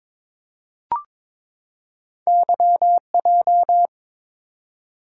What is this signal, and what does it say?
Morse code, heard as a keyed tone, about 22 wpm, 700 Hz. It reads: YJ
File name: practice / 22wpm / Two Letter-Number Combinations - 22wpm